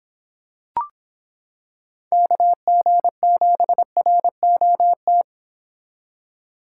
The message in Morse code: KG7ROT

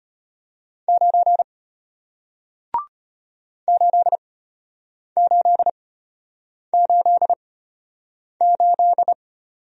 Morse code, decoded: 9 8888

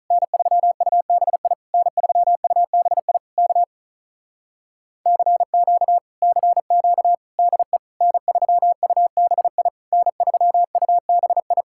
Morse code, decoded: N3UBI N3UBI K CQ CQ DE N3UBI N3UBI

